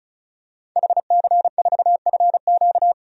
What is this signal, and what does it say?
Morse code, heard as a keyed tone, about 36 wpm, 700 Hz. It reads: HC4FQ
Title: DXCC-100 Call Signs - 5 Characters - 35wpm